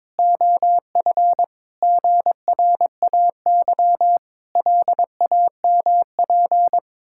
Morse code, heard as a keyed tone, about 22 wpm, 700 Hz. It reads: OF GRAY LAMP
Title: Pangram Challenge v2 - 22wpm